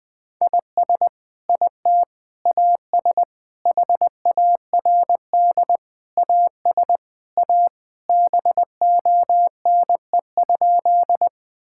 Morse code, heard as a keyed tone, about 20 wpm, 700 Hz. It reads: IS IT AS HARD AS A BONE?
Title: Sentences from Top 1600 Words 20wpm